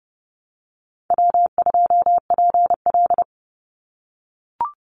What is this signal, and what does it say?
Morse code, heard as a keyed tone, about 30 wpm, 700 Hz. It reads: W2PL
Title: Call Signs - 30wpm